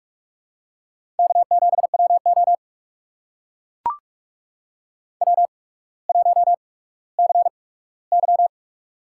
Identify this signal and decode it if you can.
Morse code, heard as a keyed tone, about 44 wpm, spaced as for 19 wpm, 690 Hz. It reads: K7WO W1CY
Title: USA Call Signs - 4 Characters - 8x Character Spacing - 45wpm